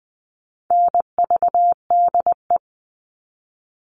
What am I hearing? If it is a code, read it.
Morse code, heard as a keyed tone, about 20 wpm, 700 Hz. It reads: NVDE
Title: All Cut Numbers - 4-Digit 20wpm